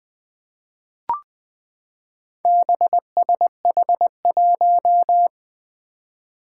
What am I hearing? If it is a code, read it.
Morse code, heard as a keyed tone, about 20 wpm, 700 Hz. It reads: BSH1
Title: Four Challenging Letter-Number Combinations 20wpm